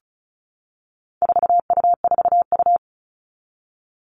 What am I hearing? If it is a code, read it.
Morse code, heard as a keyed tone, about 34 wpm, 700 Hz. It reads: 4U4U
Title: All Cut Numbers - 4-Digit 35wpm